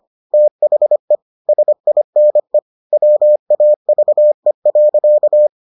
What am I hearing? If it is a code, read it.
Morse code, heard as a keyed tone, about 25 wpm, 600 Hz. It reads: THE SINE WAVE.